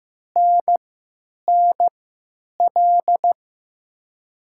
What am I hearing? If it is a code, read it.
Morse code, heard as a keyed tone, about 15 wpm, spaced as for 9 wpm, 700 Hz. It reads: NNL
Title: USA Call Signs - 6 Characters - 3x Character Spacing - 15wpm